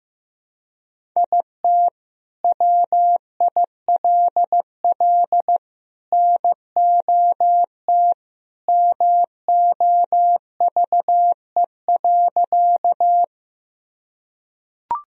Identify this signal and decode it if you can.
Morse code, heard as a keyed tone, about 15 wpm, 700 Hz. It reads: IT WILL NOT MOVE.
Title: Sentences from Top 1000 Words - Review Part 1 - 15wpm